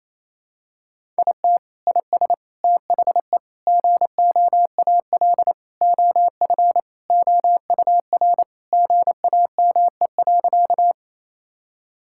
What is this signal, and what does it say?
Morse code, heard as a keyed tone, about 28 wpm, 700 Hz. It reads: IT IS THE GOAL OF OUR GAME.
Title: Sentences from Top 1000 Words - Review Part 5 - 28wpm